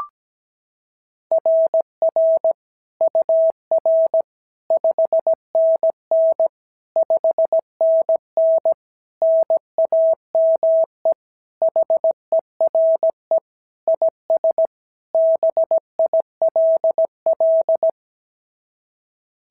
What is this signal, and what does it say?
Morse code, heard as a keyed tone, about 16 wpm, 650 Hz. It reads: RR UR 5NN 5NN NAME HERE IS BILL
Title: SOTA QSO Practice 17wpm